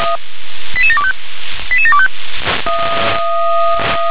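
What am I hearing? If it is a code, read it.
Five-tone selective call (ZVEI), heard as quick runs of five tones, tones of about 70 ms. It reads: 88725 88725
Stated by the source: ZVEI five-tone selective call 88713, received by SDR.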